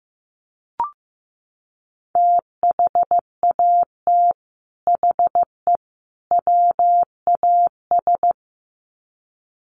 Morse code, heard as a keyed tone, about 15 wpm, 700 Hz. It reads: THAT HE WAS
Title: Sets of 3 Words 15wpm